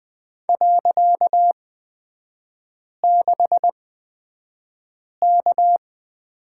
Morse code, heard as a keyed tone, about 20 wpm, spaced as for 5 wpm, 700 Hz. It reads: .6K
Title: 255 Lesson 41e - Period - Three Characters 20wpm